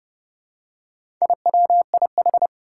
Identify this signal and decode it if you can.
Morse code, heard as a keyed tone, about 31 wpm, 700 Hz. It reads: IWIH
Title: Four Challenging Letter-Number Combinations 30wpm